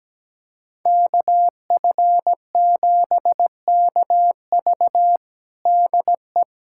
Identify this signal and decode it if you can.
Morse code, heard as a keyed tone, about 17 wpm, 700 Hz. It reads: KF7KV DE
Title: USA Callsign DE Callsign - 5 Characters - 17wpm